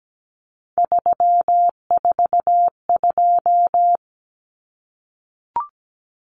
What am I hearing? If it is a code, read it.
Morse code, heard as a keyed tone, about 17 wpm, 700 Hz. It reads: 342
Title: Numbers - 3-Digit 17wpm